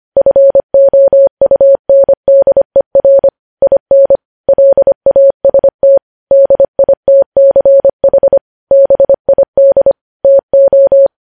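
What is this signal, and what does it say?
Morse code, heard as a keyed tone, about 25 wpm, 550 Hz. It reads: FOUNDER IN LAST DITCH BID TO
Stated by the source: News headlines sent in Morse code at 25.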